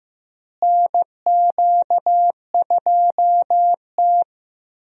Morse code, heard as a keyed tone, about 15 wpm, 700 Hz. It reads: NQ2T
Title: USA Call Signs - 4 Characters - Repeated 3x - 15wpm